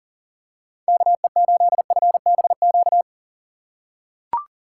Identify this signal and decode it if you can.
Morse code, heard as a keyed tone, about 40 wpm, 700 Hz. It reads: KE8FBQ
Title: USA Call Signs - 6 Characters - 6x Character Spacing - 40wpm